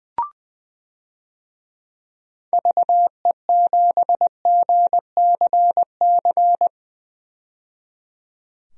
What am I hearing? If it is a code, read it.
Morse code, heard as a keyed tone, about 20 wpm, 700 Hz. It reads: VE7GCC
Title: DXCC-100 Call Signs - 6 Characters - 20wpm